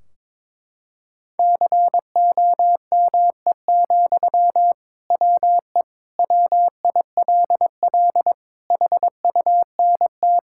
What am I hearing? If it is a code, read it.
Morse code, heard as a keyed tone, about 22 wpm, 700 Hz. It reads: COME, WE WILL HUNT
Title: Sentences from Top 1600 Words 22wpm